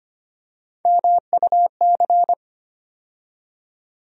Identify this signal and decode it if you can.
Morse code, heard as a keyed tone, about 25 wpm, 700 Hz. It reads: MUC
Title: USA Call Sign Postfixes - 3 Letters 25wpm